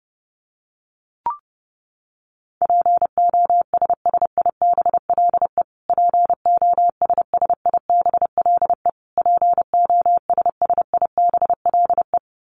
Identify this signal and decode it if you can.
Morse code, heard as a keyed tone, about 30 wpm, 700 Hz. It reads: POSSIBLE POSSIBLE POSSIBLE
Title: Top 500 Words Repeated 30wpm